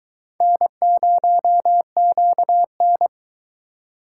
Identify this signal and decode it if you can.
Morse code, heard as a keyed tone, about 22 wpm, 700 Hz. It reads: N0QN